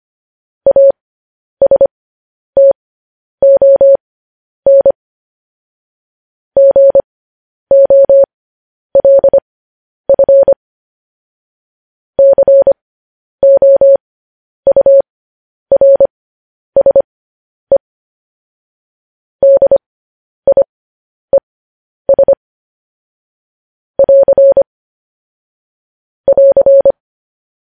Morse code, heard as a keyed tone, about 25 wpm, spaced as for 10 wpm, 550 Hz. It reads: ASTON GOLF COURSE DIES <AR> <AR>
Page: News Headlines in Morse Code at 10 WPM